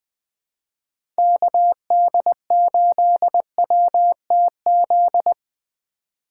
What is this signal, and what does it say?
Morse code, heard as a keyed tone, about 20 wpm, 700 Hz. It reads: KD8WTZ
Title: USA Call Signs - 2x Character Spacing - 20wpm